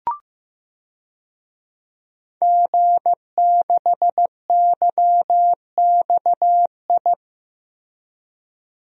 Morse code, heard as a keyed tone, about 14 wpm, 700 Hz. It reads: G6YXI
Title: DXCC-100 Call Signs - 5 Characters - Repeated 3x - 15wpm